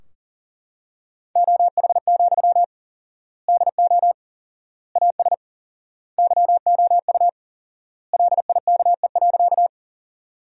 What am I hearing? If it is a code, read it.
Morse code, heard as a keyed tone, about 40 wpm, 700 Hz. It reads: OH, DO AS YOU LIKE.
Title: 4x Word-Spacing - Sentences from Top 500 Words - 40wpm